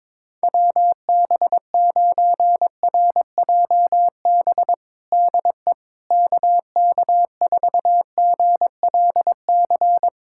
Morse code, heard as a keyed tone, about 22 wpm, 700 Hz. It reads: WB9RJB DE KK4GLC
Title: USA Callsign DE Callsign 22wpm